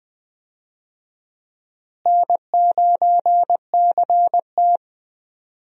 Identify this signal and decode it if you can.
Morse code, heard as a keyed tone, about 20 wpm, 700 Hz. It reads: N9CT
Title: USA Call Signs - 4 Characters - Repeated 3x - 20wpm